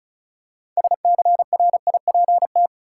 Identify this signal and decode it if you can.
Morse code, heard as a keyed tone, about 36 wpm, 700 Hz. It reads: SCRIPT
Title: Six Letter Words - 35wpm